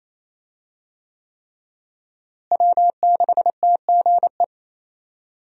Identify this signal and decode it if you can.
Morse code, heard as a keyed tone, about 28 wpm, 700 Hz. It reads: W6TGE
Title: USA Call Signs - 5 Characters - Repeated 3x - 28wpm